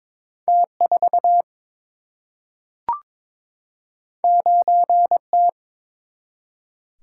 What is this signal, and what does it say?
Morse code, heard as a keyed tone, about 22 wpm, 700 Hz. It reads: T 4 9 T
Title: Single-Letter Single-Number 22wpm